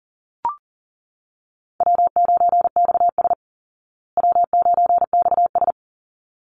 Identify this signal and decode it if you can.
Morse code, heard as a keyed tone, about 40 wpm, 700 Hz. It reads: W9XS W9XS
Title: Call Signs Repeated 3x - 40wpm